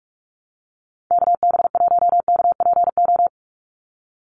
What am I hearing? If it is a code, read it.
Morse code, heard as a keyed tone, about 44 wpm, 700 Hz. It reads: KB1KPO
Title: USA Call Signs 45wpm